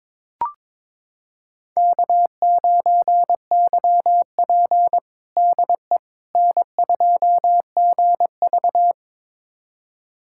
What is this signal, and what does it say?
Morse code, heard as a keyed tone, about 22 wpm, 700 Hz. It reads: K9YP DE N2GV
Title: USA Callsign DE Callsign - 4 Characters - 22wpm